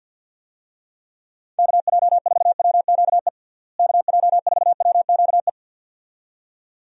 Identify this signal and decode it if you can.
Morse code, heard as a keyed tone, about 51 wpm, 700 Hz. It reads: KJ4WQE KJ4WQE
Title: USA Call Signs - 6 Characters - Repeated 2x - 50wpm